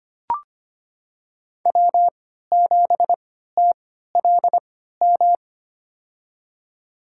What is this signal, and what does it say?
Morse code, heard as a keyed tone, about 25 wpm, spaced as for 14 wpm, 700 Hz. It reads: W7TLM
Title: USA Call Signs - 5 Characters - 3x Character Spacing - 25wpm